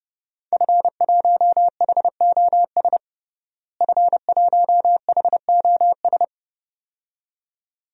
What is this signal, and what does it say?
Morse code, heard as a keyed tone, about 32 wpm, 700 Hz. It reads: F1HOS F1HOS
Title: DXCC-100 Call Signs - 5 Characters - Repeated 2x - 30wpm